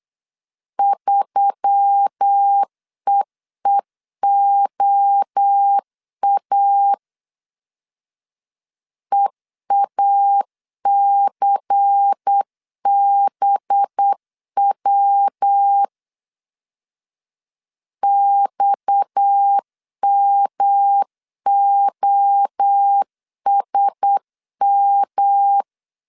Morse code, heard as a keyed tone, about 8 wpm, 800 Hz. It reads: 3EEOA EACBW XMOSM
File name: 8wpm